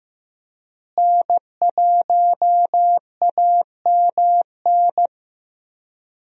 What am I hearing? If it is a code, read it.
Morse code, heard as a keyed tone, about 15 wpm, 700 Hz. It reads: N1AMN
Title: USA Call Signs - 5 Characters - Repeated 3x - 15wpm